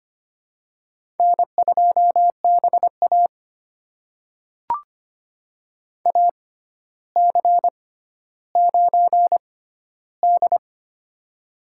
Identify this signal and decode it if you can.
Morse code, heard as a keyed tone, about 25 wpm, spaced as for 13 wpm, 700 Hz. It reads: N2BA AC9D
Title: USA Call Signs - 4 Characters - 6x Character Spacing - 25wpm